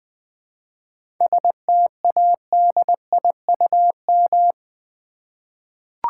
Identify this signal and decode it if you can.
Morse code, heard as a keyed tone, about 20 wpm, 700 Hz. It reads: STADIUM
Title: 7 Letter Words - First 3 Letters Given - 20wpm